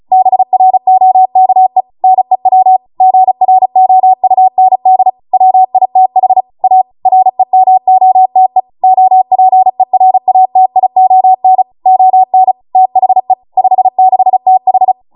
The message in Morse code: BROKE NEW GROUND WITH A REMOTE OPERATION ON THE 56TH